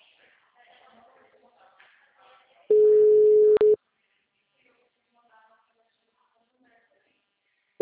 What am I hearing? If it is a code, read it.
Ringback tone (Europe, 425 Hz), with speech.